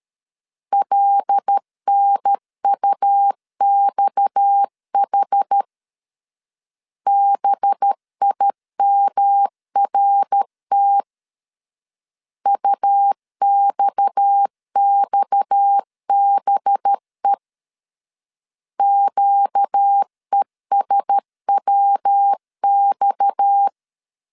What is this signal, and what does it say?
Morse code, heard as a keyed tone, about 13 wpm, 800 Hz. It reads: LNUXH BIMRT UXXBE QESWX
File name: LEZIONI / 6 PKX0 / 6 PKX 12wpm 2024